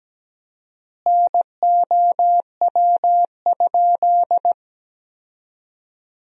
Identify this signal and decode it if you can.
Morse code, heard as a keyed tone, about 17 wpm, 700 Hz. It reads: NOW?
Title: 4x Word-Spacing - Sentences from Top 200 Words - 17wpm